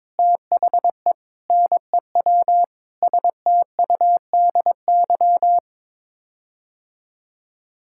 Morse code, heard as a keyed tone, about 22 wpm, 700 Hz. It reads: THE NEW STUDY
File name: practice / 22wpm / Sets of 8 Words - 22wpm